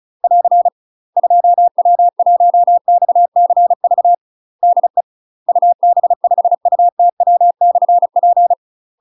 Morse code, heard as a keyed tone, about 36 wpm, 700 Hz. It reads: <AR> 2W1XCV DE UB5UTW/P
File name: UK MASTER QSOs TRAINING / 35WPM / UK MASTER QSOs 0011